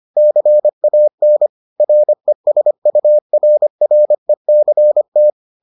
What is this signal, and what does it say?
Morse code, heard as a keyed tone, about 25 wpm, 600 Hz. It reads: CAN RESURRECT